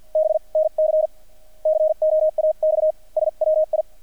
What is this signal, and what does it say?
Morse code, heard as a keyed tone, about 50 wpm, spaced as for 41 wpm, 620 Hz. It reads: BNC COAX SWI
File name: archived / CRSnet